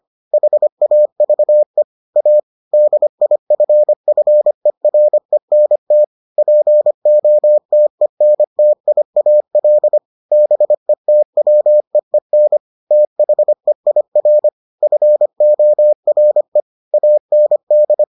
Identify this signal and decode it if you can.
Morse code, heard as a keyed tone, about 25 wpm, 600 Hz. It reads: HAVE A DIFFERENT POTENTIAL BETWEEN THEIR FORE AND